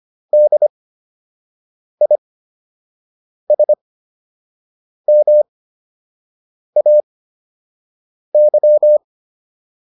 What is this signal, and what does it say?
Morse code, heard as a keyed tone, about 25 wpm, spaced as for 6 wpm, 600 Hz. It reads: DISMAY